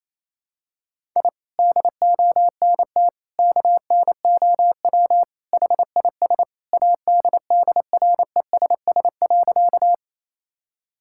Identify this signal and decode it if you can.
Morse code, heard as a keyed tone, about 28 wpm, 700 Hz. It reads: I DONT KNOW HIS ADDRESS.
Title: Sentences from Top 1000 Words - Review Part 6 - 28wpm